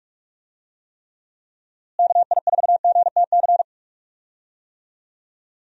Morse code, heard as a keyed tone, about 45 wpm, 700 Hz. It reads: KI4GTC
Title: USA Call Signs - 6 Characters - Repeated 2x - 45wpm